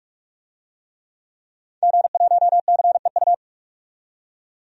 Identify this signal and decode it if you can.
Morse code, heard as a keyed tone, about 47 wpm, 700 Hz. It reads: G1CEU
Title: DXCC-100 Call Signs - 5 Characters - 45wpm